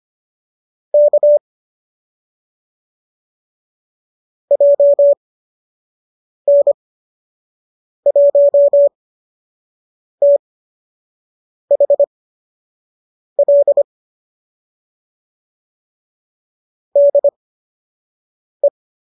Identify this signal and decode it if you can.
Morse code, heard as a keyed tone, about 25 wpm, spaced as for 6 wpm, 570 Hz. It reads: K JN1THL DE
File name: qso003 06